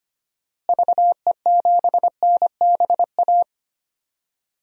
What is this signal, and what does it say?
Morse code, heard as a keyed tone, about 25 wpm, 700 Hz. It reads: VE7NBA
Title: DXCC-100 Call Signs - 6 Characters - Repeated 2x - 25wpm